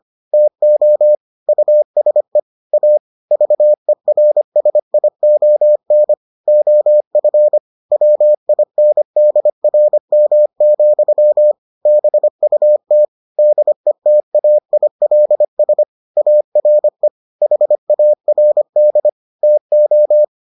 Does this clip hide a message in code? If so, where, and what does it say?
Yes: Morse code from 0.3 to 20.2 s, heard as a keyed tone, about 25 wpm, 600 Hz. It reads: TO USE A VERSION OF WINDRM, BUT DETAILS ARE HARD TO